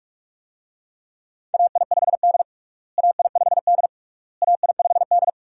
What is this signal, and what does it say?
Morse code, heard as a keyed tone, about 41 wpm, 700 Hz. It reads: AI5D AI5D AI5D
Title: USA Call Signs - 4 Characters - Repeated 3x - 45wpm